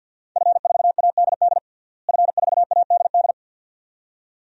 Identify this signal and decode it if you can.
Morse code, heard as a keyed tone, about 45 wpm, 700 Hz. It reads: F4ADD F4ADD
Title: DXCC-100 Call Signs - 5 Characters - Repeated 2x - 50wpm